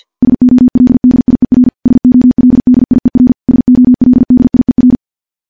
Morse code, heard as a keyed tone, about 51 wpm, 250 Hz. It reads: SOLDIER SOLDIER SOLDIER
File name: morse / morse-training-50WPM